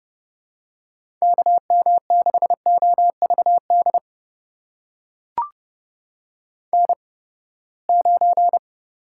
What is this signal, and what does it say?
Morse code, heard as a keyed tone, about 30 wpm, spaced as for 21 wpm, 700 Hz. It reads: KM6OVD N9
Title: USA Call Signs - 8x Character Spacing - 30wpm